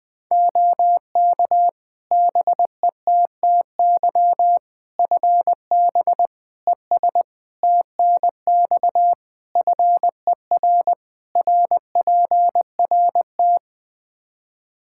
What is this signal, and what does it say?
Morse code, heard as a keyed tone, about 20 wpm, 700 Hz. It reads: OK BETTY FB ES TNX FER RPRT
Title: Practice QSO - Exchange 2 - Info - 20wpm